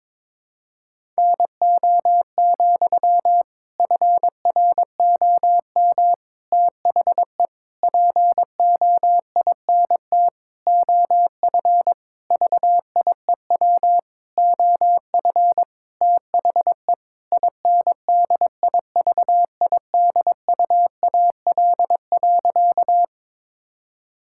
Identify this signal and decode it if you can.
Morse code, heard as a keyed tone, about 22 wpm, 700 Hz. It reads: NO, FROM THE POINT OF VIEW OF THE INDIVIDUAL.
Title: Sentences from Top 1000 Words - Review Part 6 - 22wpm